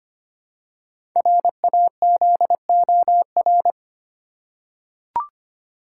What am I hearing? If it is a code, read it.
Morse code, heard as a keyed tone, about 25 wpm, 690 Hz. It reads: RAZOR
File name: practice / 25wpm / Easy Words with Hints - 25wpm